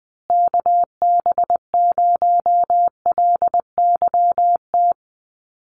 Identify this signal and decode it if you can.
Morse code, heard as a keyed tone, about 19 wpm, 700 Hz. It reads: KB0LYT
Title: USA Call Signs Repeated 3x 20wpm